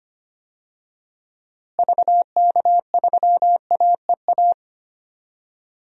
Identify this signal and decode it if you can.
Morse code, heard as a keyed tone, about 25 wpm, 700 Hz. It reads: VK3AEA
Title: DXCC-100 Call Signs - 6 Characters - Repeated 2x - 25wpm